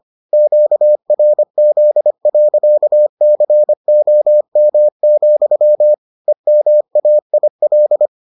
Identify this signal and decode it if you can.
Morse code, heard as a keyed tone, about 25 wpm, 600 Hz. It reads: QRZ.COM, EMAIL